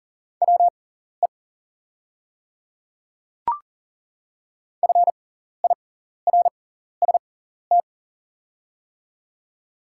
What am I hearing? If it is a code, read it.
Morse code, heard as a keyed tone, about 40 wpm, spaced as for 14 wpm, 700 Hz. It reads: WE FIRST
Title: 6x Character Spacing - Top 100 Words - 40wpm